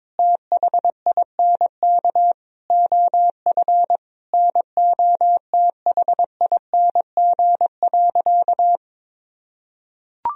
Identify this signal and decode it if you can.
Morse code, heard as a keyed tone, about 22 wpm, 700 Hz. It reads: THINK OF NOTHING.
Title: Sentences from Top 1000 Words - Review Part 3 - 22wpm